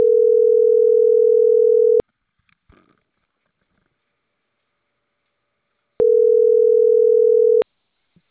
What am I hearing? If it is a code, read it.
Ringback tone (North America), 440 + 480 Hz.